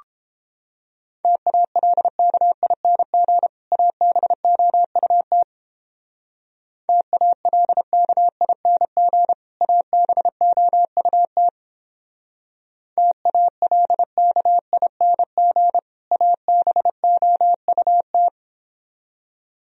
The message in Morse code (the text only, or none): TALKING ABOUT TALKING ABOUT TALKING ABOUT